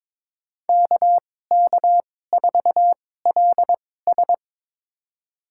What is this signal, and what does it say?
Morse code, heard as a keyed tone, about 22 wpm, spaced as for 16 wpm, 700 Hz. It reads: KK4LS
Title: USA Call Signs - 5 Characters - 2x Character Spacing - 22wpm